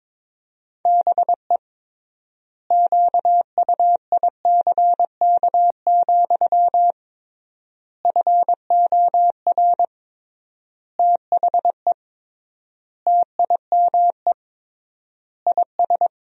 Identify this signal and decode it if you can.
Morse code, heard as a keyed tone, about 21 wpm, 700 Hz. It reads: BE QUICK, FOR THE TIME IS
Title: 3x Word-Spacing - Sentences from Top 500 Words - 22wpm